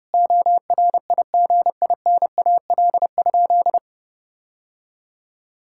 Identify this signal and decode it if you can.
Morse code, heard as a keyed tone, about 30 wpm, 700 Hz. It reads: ORIGINAL?
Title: Sentences from Top 1000 Words - Review Part 6 - 30wpm